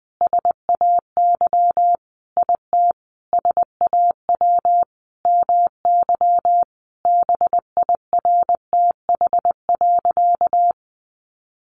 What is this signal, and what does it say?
Morse code, heard as a keyed tone, about 20 wpm, 700 Hz. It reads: SAY IT SAW MY BIRTH.
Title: Sentences from Top 800 Words 20wpm